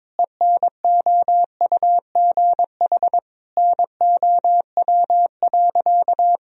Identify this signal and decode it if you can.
Morse code, heard as a keyed tone, about 22 wpm, 700 Hz. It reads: ENOUGH NOW.
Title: Sentences from Top 1000 Words - Review Part 2 - 22wpm